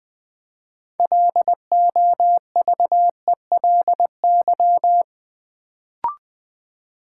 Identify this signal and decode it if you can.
Morse code, heard as a keyed tone, about 20 wpm, 700 Hz. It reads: LOVELY
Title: Build A Word - 2 Letters At A Time - 6-Letter Words - 20wpm